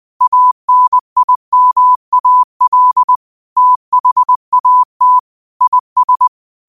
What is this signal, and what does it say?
Morse code, heard as a keyed tone, about 20 wpm, 1000 Hz. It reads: ANIMAL THAT IS
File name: qotdfiles / 20wpm